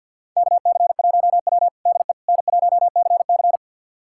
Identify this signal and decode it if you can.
Morse code, heard as a keyed tone, about 47 wpm, 700 Hz. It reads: KC1W DE N1CC
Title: USA Callsign DE Callsign - 4 Characters - 50wpm